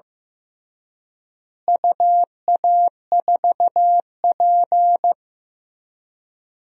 Morse code, heard as a keyed tone, about 16 wpm, 700 Hz. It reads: UA4P
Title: DXCC-100 Call Signs - 4 Characters - Repeated 2x - 15wpm